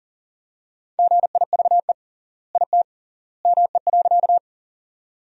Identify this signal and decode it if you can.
Morse code, heard as a keyed tone, about 40 wpm, 700 Hz. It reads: GIVE IT ME.
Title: 3x Word-Spacing - Sentences from Top 200 Words - 40wpm